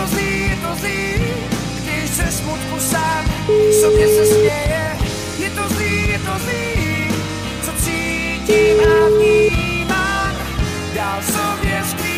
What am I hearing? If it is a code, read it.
Ringback tone (Europe, 425 Hz), with noise.